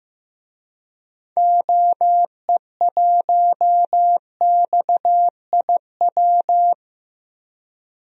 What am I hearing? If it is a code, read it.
Morse code, heard as a keyed tone, about 15 wpm, 700 Hz. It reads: OE1XIW
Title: DXCC-100 Call Signs - 6 Characters - 15wpm